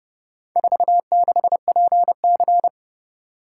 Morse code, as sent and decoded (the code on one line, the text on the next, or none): ....- -.... .--. -.-.
46PC